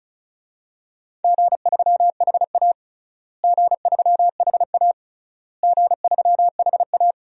Morse code, heard as a keyed tone, about 36 wpm, 700 Hz. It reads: G3HA G3HA G3HA
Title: DXCC-100 Call Signs - 4 Characters - Repeated 3x - 35wpm